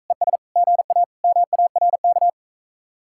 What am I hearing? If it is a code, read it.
Morse code, heard as a keyed tone, about 44 wpm, 700 Hz. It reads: ES GA MARK